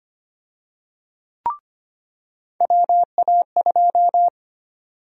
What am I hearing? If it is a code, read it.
Morse code, heard as a keyed tone, about 25 wpm, 700 Hz. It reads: WA2
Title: USA Call Sign Prefixes - 25wpm